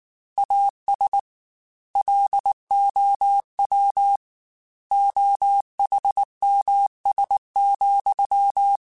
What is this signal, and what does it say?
Morse code, heard as a keyed tone, about 19 wpm, 800 Hz. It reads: AS LOW OHMS,
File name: Technical Subjects / Technical Subjects-mp3s / XC @18